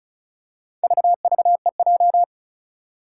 Morse code, heard as a keyed tone, about 36 wpm, 700 Hz. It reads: VVEJ
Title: Four Challenging Letter-Number Combinations 35wpm